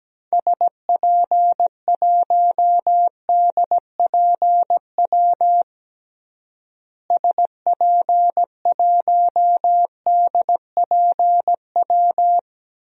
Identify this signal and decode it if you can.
Morse code, heard as a keyed tone, about 17 wpm, 700 Hz. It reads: SP1DPW SP1DPW
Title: DXCC-100 Call Signs - 6 Characters - Repeated 2x - 17wpm